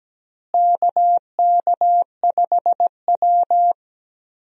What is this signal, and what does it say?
Morse code, heard as a keyed tone, about 17 wpm, 700 Hz. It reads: KK5W
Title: USA Call Signs - 4 Characters - Repeated 2x - 17wpm